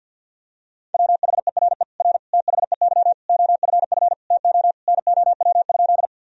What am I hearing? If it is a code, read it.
Morse code, heard as a keyed tone, about 48 wpm, 700 Hz. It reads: WHERE R THEY OFF TO NOW?
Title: Sentences from Top 200 Words and Abbreviations 50wpm